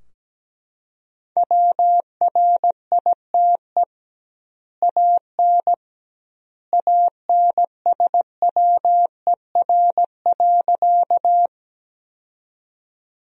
Morse code, heard as a keyed tone, about 17 wpm, 700 Hz. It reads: WRITE AN ANSWER.
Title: 2x Word-Spacing - Sentences from Top 300 Words - 17wpm